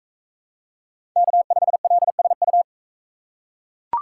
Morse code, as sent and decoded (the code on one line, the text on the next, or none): -.- ..... .-.. ... ..-
K5LSU